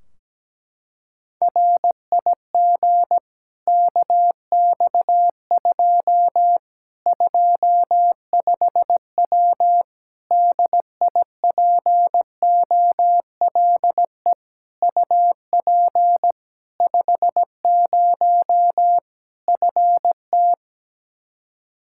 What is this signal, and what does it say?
Morse code, heard as a keyed tone, about 17 wpm, 700 Hz. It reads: RIG KX2 25W DIPOLE UP 50 FT
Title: K1USN SST QSO Practice 17wpm